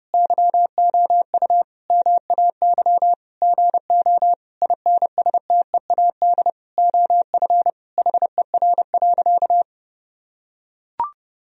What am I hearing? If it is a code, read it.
Morse code, heard as a keyed tone, about 30 wpm, 700 Hz. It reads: YOU MAY GO INSTEAD OF HER.
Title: Sentences from Top 1000 Words - Review Part 4 - 30wpm